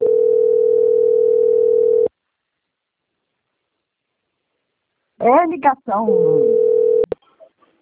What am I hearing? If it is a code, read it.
Ringback tone (North America), 440 + 480 Hz, with speech.
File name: out- - - - - .444